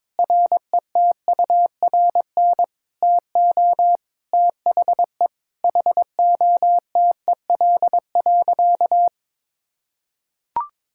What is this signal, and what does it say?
Morse code, heard as a keyed tone, about 22 wpm, 700 Hz. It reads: RETURN TO THE HOTEL.
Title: Sentences from Top 1600 Words 22wpm